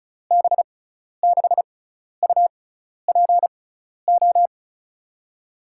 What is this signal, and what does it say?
Morse code, heard as a keyed tone, about 36 wpm, spaced as for 12 wpm, 700 Hz. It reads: B6UPO
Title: USA Call Signs - 6 Characters - 6x Character Spacing - 35wpm